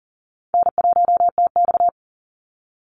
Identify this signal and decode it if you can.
Morse code, heard as a keyed tone, about 40 wpm, 700 Hz. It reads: N1TX
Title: Easy Call Signs Repeated 3x - 40wpm